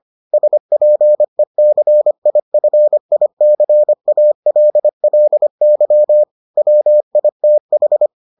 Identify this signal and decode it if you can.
Morse code, heard as a keyed tone, about 25 wpm, 600 Hz. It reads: SPECIFICALLY WITH